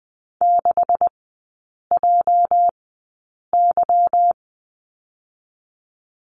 Morse code, heard as a keyed tone, about 20 wpm, spaced as for 8 wpm, 690 Hz. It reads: 6JY
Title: 205 Lesson 33e - Letter j - Three Characters 20wpm